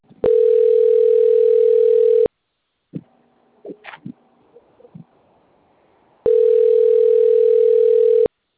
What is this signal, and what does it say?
Ringback tone (North America), 440 + 480 Hz.